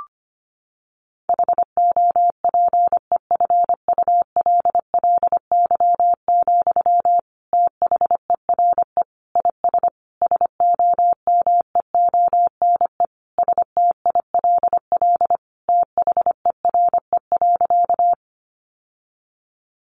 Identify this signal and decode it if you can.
Morse code, heard as a keyed tone, about 25 wpm, 700 Hz. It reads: HOPEFULLY, THERE IS SOMEONE STILL THERE.
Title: Sentences from Top 1500 Words 25wpm